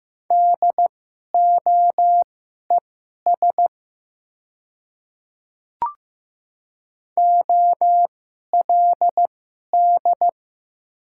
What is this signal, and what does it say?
Morse code, heard as a keyed tone, about 15 wpm, spaced as for 11 wpm, 700 Hz. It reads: DOES OLD
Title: 2x Character Spacing - Top 200 Words - 15wpm